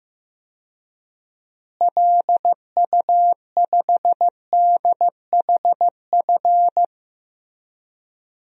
Morse code, heard as a keyed tone, about 15 wpm, 700 Hz. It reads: LU5DHF
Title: DXCC-100 Call Signs - 6 Characters - Repeated 3x - 15wpm